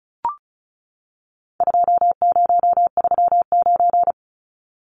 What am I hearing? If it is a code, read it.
Morse code, heard as a keyed tone, about 34 wpm, 700 Hz. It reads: 2039
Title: Numbers - 4-Digit 35wpm